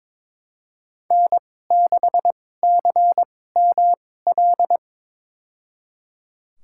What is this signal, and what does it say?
Morse code, heard as a keyed tone, about 22 wpm, spaced as for 16 wpm, 700 Hz. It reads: N6CML
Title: USA Call Signs - 5 Characters - 2x Character Spacing - 22wpm